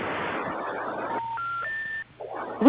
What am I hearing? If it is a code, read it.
Special information tone, with noise.